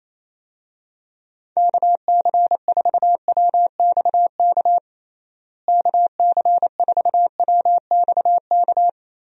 Morse code, heard as a keyed tone, about 28 wpm, 700 Hz. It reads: KC4WXK KC4WXK
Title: USA Call Signs - 6 Characters - Repeated 3x - 28wpm